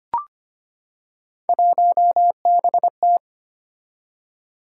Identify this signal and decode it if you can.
Morse code, heard as a keyed tone, about 25 wpm, 700 Hz. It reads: 1BT
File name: Three Letter-Number Combinations - 25wpm